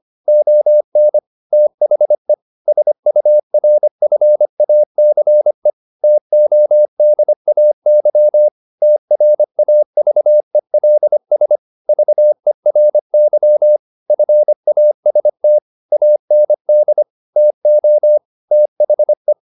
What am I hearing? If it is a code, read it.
Morse code, heard as a keyed tone, about 25 wpm, 600 Hz. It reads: ON THE SURFACE TODAY TRAVELS VERY FAST AND TO THE
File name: the old man and the sea part 1 25wpm